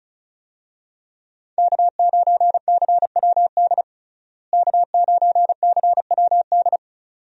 Morse code, heard as a keyed tone, about 34 wpm, 700 Hz. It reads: K9CWD K9CWD
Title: USA Call Signs - 5 Characters - Repeated 2x - 35wpm